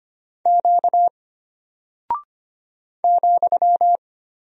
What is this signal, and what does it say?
Morse code, heard as a keyed tone, about 25 wpm, 700 Hz. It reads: Q ,
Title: Single Character (Letter:Number:Punctuation:Prosign) Alt 25wpm